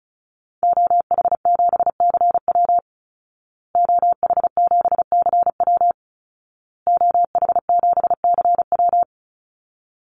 Morse code, heard as a keyed tone, about 34 wpm, 700 Hz. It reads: OH7CW OH7CW OH7CW
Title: Call Signs Repeated 2x - 35wpm